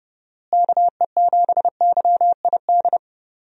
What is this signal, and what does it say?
Morse code, heard as a keyed tone, about 30 wpm, 700 Hz. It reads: KE7YID
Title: USA Call Signs - 6 Characters - Repeated 3x - 30wpm